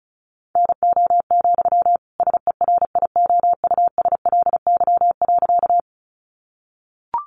Morse code, heard as a keyed tone, about 34 wpm, 700 Hz. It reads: NO, SERIOUSLY.
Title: Sentences from Top 1200 Words 35wpm